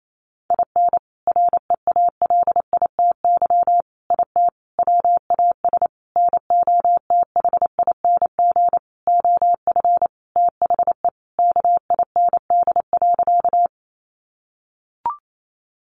Morse code, heard as a keyed tone, about 28 wpm, 700 Hz. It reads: IN REALITY IT WAS NOTHING OF THE KIND.
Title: Sentences from Top 1200 Words 28wpm